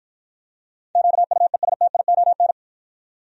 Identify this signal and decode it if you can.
Morse code, heard as a keyed tone, about 53 wpm, 700 Hz. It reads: QUESTION